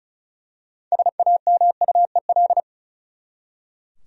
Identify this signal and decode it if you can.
Morse code, heard as a keyed tone, about 34 wpm, 700 Hz. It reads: SAMUEL
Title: USA Common Names - 35wpm